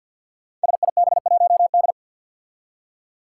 Morse code, heard as a keyed tone, about 53 wpm, 700 Hz. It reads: SIB1D
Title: Five Challenging Letter-Number Combinations 50wpm